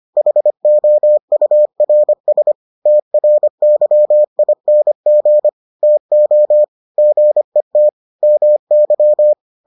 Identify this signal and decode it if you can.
Morse code, heard as a keyed tone, about 25 wpm, 600 Hz. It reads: HOURS TRYING TO GET MY